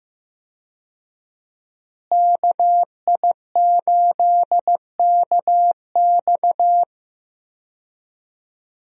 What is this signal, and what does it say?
Morse code, heard as a keyed tone, about 15 wpm, 700 Hz. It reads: KI8KX
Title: USA Call Signs - 5 Characters - Repeated 3x - 15wpm